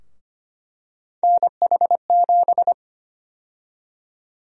Morse code, heard as a keyed tone, about 25 wpm, 700 Hz. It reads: NH7
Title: USA Call Sign Prefixes - Repeated 2x - 25wpm